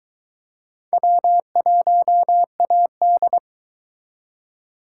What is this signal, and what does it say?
Morse code, heard as a keyed tone, about 22 wpm, 700 Hz. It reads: W1AD